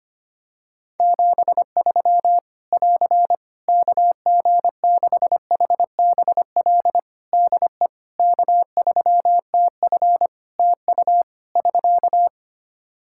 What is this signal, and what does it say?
Morse code, heard as a keyed tone, about 25 wpm, 700 Hz. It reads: 73 <AR> KG6HBL DE K3TF TU <SK>
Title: Practice QSO - Exchange 4 - Ending - 25wpm